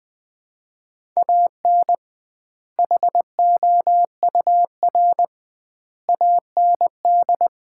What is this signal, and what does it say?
Morse code, heard as a keyed tone, about 20 wpm, 700 Hz. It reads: AN HOUR AND
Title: 2x Word-Spacing - Sets of 5 Words - 20wpm